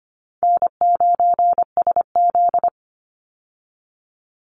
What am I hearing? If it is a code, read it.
Morse code, heard as a keyed tone, about 25 wpm, 700 Hz. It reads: N9SZ
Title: Easy Call Signs Repeated 2x - 25wpm